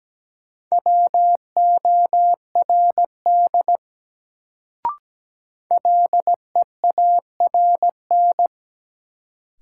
Morse code, heard as a keyed tone, about 17 wpm, 700 Hz. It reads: WORD LEARN
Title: Sentences from Top 300 Words - New Word Drill - 17wpm